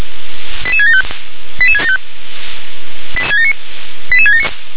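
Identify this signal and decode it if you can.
Five-tone selective call (ZVEI), heard as quick runs of five tones, tones of about 70 ms. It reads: 88675 88675 88679
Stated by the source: ZVEI five-tone selective call received via SDR.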